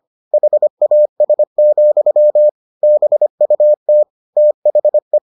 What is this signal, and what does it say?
Morse code, heard as a keyed tone, about 25 wpm, 600 Hz. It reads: HAS, BUT THE